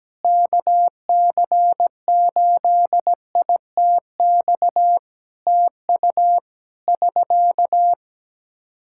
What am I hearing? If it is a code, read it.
Morse code, heard as a keyed tone, about 17 wpm, 700 Hz. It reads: KC8ITX TU <SK>
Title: K1USN SST QSO Practice 17wpm